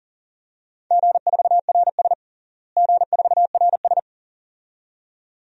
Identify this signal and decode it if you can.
Morse code, heard as a keyed tone, about 40 wpm, 700 Hz. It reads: G4RS G4RS
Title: DXCC-100 Call Signs - 4 Characters - Repeated 2x - 40wpm